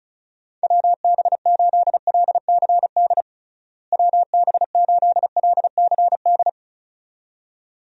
Morse code, heard as a keyed tone, about 36 wpm, 700 Hz. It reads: WB8LCD WB8LCD
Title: USA Call Signs - 6 Characters - Repeated 2x - 35wpm